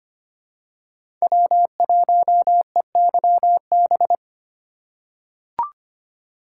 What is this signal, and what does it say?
Morse code, heard as a keyed tone, about 25 wpm, 700 Hz. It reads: W1EYB
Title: USA Call Signs - 5 Characters - 3x Character Spacing - 25wpm